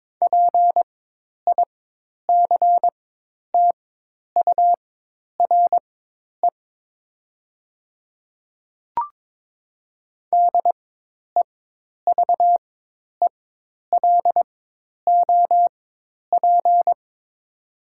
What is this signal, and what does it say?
Morse code, heard as a keyed tone, about 22 wpm, spaced as for 10 wpm, 700 Hz. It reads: PICTURE DEVELOP
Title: 4x Character Spacing - Top 500 Words - 22wpm